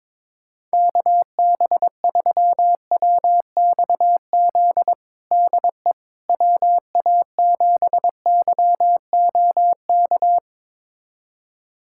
Morse code, heard as a keyed tone, about 22 wpm, 700 Hz. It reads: KB3WXZ DE WA7YOK
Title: USA Callsign DE Callsign - 6 Characters - 22wpm